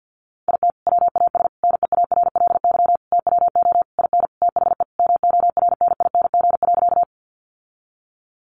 Morse code, heard as a keyed tone, about 50 wpm, 700 Hz. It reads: IT WAS NEARLY TWO IN THE MORNING.
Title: Sentences from Top 1300 Words 50wpm